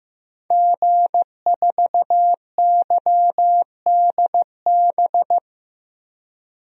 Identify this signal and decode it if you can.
Morse code, heard as a keyed tone, about 15 wpm, 700 Hz. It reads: G4YDB
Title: DXCC-100 Call Signs - 5 Characters - Repeated 2x - 15wpm